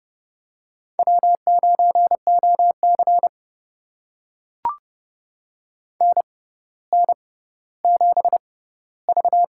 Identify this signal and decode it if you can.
Morse code, heard as a keyed tone, about 30 wpm, spaced as for 15 wpm, 700 Hz. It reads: W9OC NN7V
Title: USA Call Signs - 6x Character Spacing - 30wpm